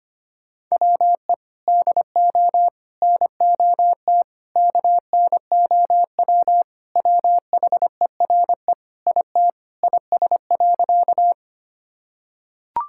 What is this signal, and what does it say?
Morse code, heard as a keyed tone, about 25 wpm, 700 Hz. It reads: WE DO NOT KNOW WHERE IT IS.
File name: practice / 25wpm / Sentences with 4-Word Phrases - 25wpm